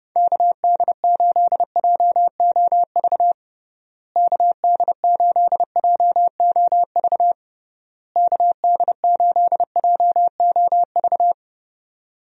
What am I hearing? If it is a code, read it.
Morse code, heard as a keyed tone, about 32 wpm, 700 Hz. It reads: KD8JOV KD8JOV KD8JOV
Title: USA Call Signs - 6 Characters - Repeated 3x - 30wpm